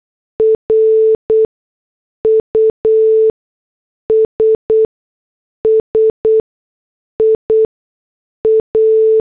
Morse code, heard as a keyed tone, about 8 wpm, spaced as for 6 wpm, 440 Hz. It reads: RUSSIA
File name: morse / headlines